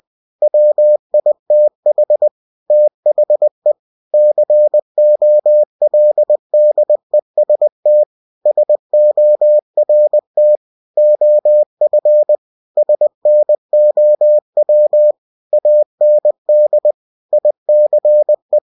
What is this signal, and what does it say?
Morse code, heard as a keyed tone, about 20 wpm, 600 Hz. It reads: WITH THE COLDEST SORT OF SNOW AND ICE
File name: cw-books / winnie-the-pooh / winnie-20wpm-page-0093